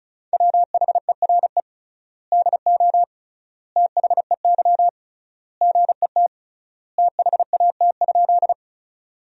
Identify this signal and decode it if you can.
Morse code, heard as a keyed tone, about 36 wpm, 700 Hz. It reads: WHERE DO THEY GET THAT?
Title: 3x Word-Spacing - Sentences from Top 200 Words - 35wpm